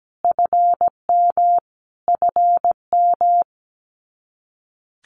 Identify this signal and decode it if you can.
Morse code, heard as a keyed tone, about 17 wpm, 700 Hz. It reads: FM FM
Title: Abbreviations Repeated 3x - 17wpm